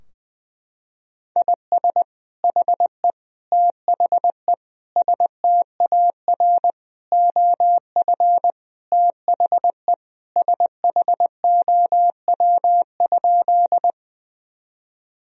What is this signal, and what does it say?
Morse code, heard as a keyed tone, about 20 wpm, 700 Hz. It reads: IS HE THE STAR OF THE SHOW?
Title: Sentences from Top 1000 Words - Review Part 2 - 20wpm